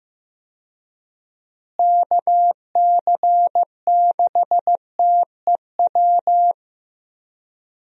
Morse code, heard as a keyed tone, about 15 wpm, 700 Hz. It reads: KC6TEW
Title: USA Call Signs - 6 Characters - Repeated 3x - 15wpm